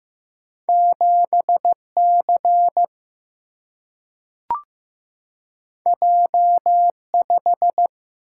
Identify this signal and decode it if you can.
Morse code, heard as a keyed tone, about 15 wpm, 700 Hz. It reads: 7C J5
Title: Single-Letter Single-Number 15wpm